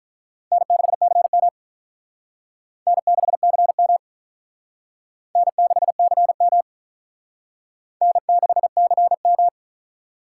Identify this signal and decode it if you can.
Morse code, heard as a keyed tone, about 42 wpm, 700 Hz. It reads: N6CM N6CM N6CM N6CM
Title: Speed Racing - USA 1x2 Call Signs 35wpm